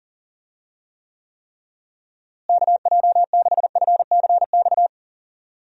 Morse code, heard as a keyed tone, about 40 wpm, 700 Hz. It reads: KJ6FCX
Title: USA Call Signs - 6 Characters - Repeated 3x - 40wpm